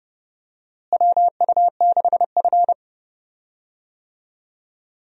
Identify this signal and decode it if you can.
Morse code, heard as a keyed tone, about 31 wpm, 700 Hz. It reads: WU6F
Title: USA Call Signs - 4 Characters - Repeated 3x - 30wpm